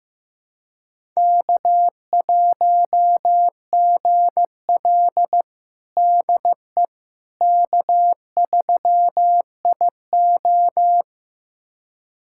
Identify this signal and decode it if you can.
Morse code, heard as a keyed tone, about 16 wpm, 700 Hz. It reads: K1GL DE K3IO
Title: USA Callsign DE Callsign - 4 Characters - 15wpm